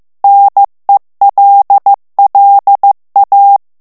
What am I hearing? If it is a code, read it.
Morse code, heard as a keyed tone, about 15 wpm, 800 Hz. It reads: NELLA
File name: morse mp3 / ITALIEN texte clair 2 vitesse 14 mots minute